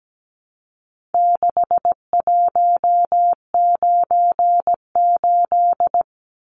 Morse code, heard as a keyed tone, about 17 wpm, 700 Hz. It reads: 6198
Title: Numbers - 4-Digit 17wpm